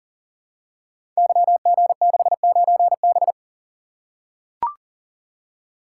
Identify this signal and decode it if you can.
Morse code, heard as a keyed tone, about 39 wpm, 700 Hz. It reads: YG69B
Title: Code Groups - 5 Letter:Numbers - 40wpm